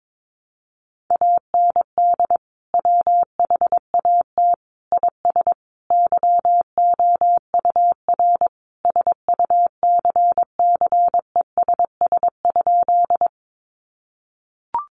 Morse code, heard as a keyed tone, about 22 wpm, 700 Hz. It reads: AND WHAT IS YOUR SUCCESS?
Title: Sentences from Top 900 Words 22wpm